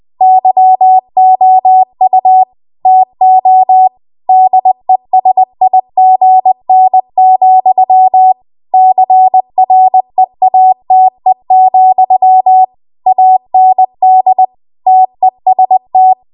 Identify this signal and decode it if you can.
Morse code, heard as a keyed tone, about 20 wpm, 750 Hz. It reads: YOU TO DESIGN, CREATE, AND TEST